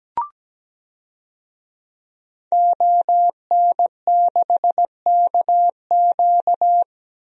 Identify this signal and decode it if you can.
Morse code, heard as a keyed tone, about 17 wpm, 700 Hz. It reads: ON6KQ
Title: DXCC-100 Call Signs - 5 Characters - 17wpm